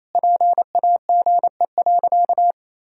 Morse code, heard as a keyed tone, about 28 wpm, 700 Hz. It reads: PAGE.
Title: Sentences from Top 1000 Words - Review Part 1 - 28wpm